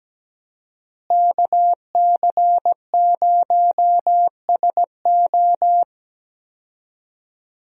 Morse code, heard as a keyed tone, about 17 wpm, 700 Hz. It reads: KC0SO
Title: USA Call Signs - 5 Characters - Repeated 2x - 17wpm